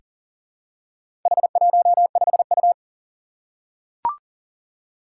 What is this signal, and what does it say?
Morse code, heard as a keyed tone, about 43 wpm, 700 Hz. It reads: H15U
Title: Four Challenging Letter-Number Combinations 40wpm